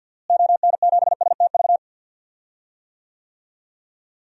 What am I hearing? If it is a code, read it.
Morse code, heard as a keyed tone, about 47 wpm, 700 Hz. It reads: ON7STV
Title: DXCC-100 Call Signs - 6 Characters - Repeated 2x - 50wpm